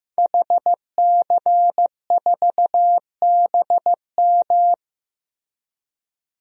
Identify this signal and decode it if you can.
Morse code, heard as a keyed tone, about 15 wpm, 700 Hz. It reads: HC4BM
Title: DXCC-100 Call Signs - 5 Characters - Repeated 3x - 15wpm